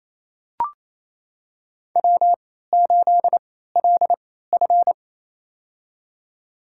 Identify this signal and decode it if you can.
Morse code, heard as a keyed tone, about 28 wpm, spaced as for 16 wpm, 700 Hz. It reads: W8LF
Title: USA Call Signs - 4 Characters - 3x Character Spacing - 28wpm